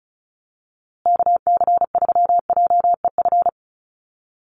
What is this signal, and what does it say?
Morse code, heard as a keyed tone, about 34 wpm, 700 Hz. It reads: KC3JEF
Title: USA Call Signs Repeated 3x 35wpm